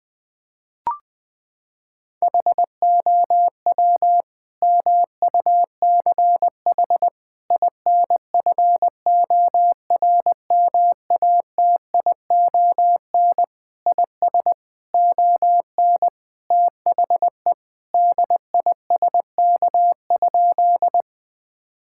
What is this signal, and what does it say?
Morse code, heard as a keyed tone, about 20 wpm, 700 Hz. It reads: HOW MUCH INFORMATION IS ON THE DISK?
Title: Sentences from Top 1000 Words - Review Part 6 - 20wpm